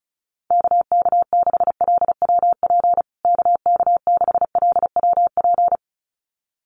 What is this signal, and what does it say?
Morse code, heard as a keyed tone, about 34 wpm, 700 Hz. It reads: KK6LWP KK6LWP
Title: USA Call Signs Repeated 2x 35wpm